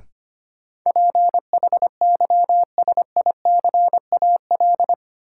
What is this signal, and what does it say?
Morse code, heard as a keyed tone, about 25 wpm, 700 Hz. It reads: PHYSICAL
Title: Sentences from Top 1400 Words - New Word Drill Alt - 25wpm